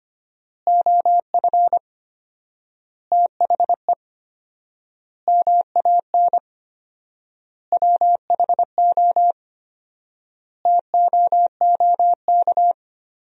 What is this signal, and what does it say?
Morse code, heard as a keyed tone, about 25 wpm, 700 Hz. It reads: OF THE MAN WHO TOOK
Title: 4x Word-Spacing - Sentences from Top 300 Words - 25wpm